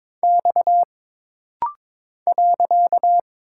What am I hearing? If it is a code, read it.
Morse code, heard as a keyed tone, about 22 wpm, 700 Hz. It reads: X .
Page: This rapid-fire timing is geared to push your ICR (Instant Character Recognition) skills.